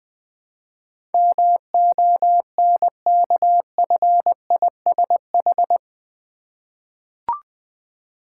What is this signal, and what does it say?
Morse code, heard as a keyed tone, about 20 wpm, 700 Hz. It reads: MONKFISH
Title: Difficult Words with Hints - 20wpm